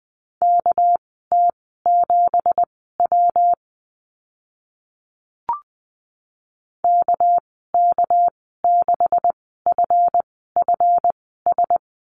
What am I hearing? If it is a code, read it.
Morse code, heard as a keyed tone, about 20 wpm, spaced as for 14 wpm, 700 Hz. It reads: KT7W KK6FFS
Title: 238 Lesson 38g - Letter q - Call Signs 20wpm